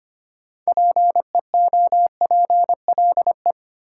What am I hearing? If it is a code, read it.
Morse code, heard as a keyed tone, about 25 wpm, 700 Hz. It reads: PEOPLE
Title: Sentences from Top 200 Words - New Word Drill - 25wpm